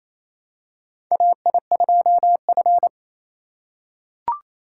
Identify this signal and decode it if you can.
Morse code, heard as a keyed tone, about 28 wpm, 700 Hz. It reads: AI2F
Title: USA Call Signs - 4 Characters - 4x Character Spacing - 28wpm